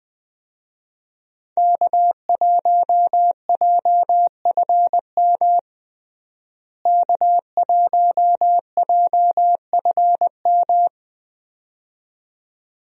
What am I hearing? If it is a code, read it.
Morse code, heard as a keyed tone, about 20 wpm, 700 Hz. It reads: K1JFM K1JFM
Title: USA Call Signs - 5 Characters - Repeated 2x - 20wpm